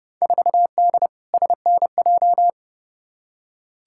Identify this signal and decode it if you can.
Morse code, heard as a keyed tone, about 30 wpm, 700 Hz. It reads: 4D SNJ
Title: Field Day Exchanges Encore - 30wpm